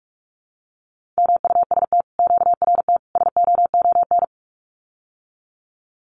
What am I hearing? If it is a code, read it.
Morse code, heard as a keyed tone, about 46 wpm, 700 Hz. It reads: MUST QRT SOON
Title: Common QSO Phrases Encore 45wpm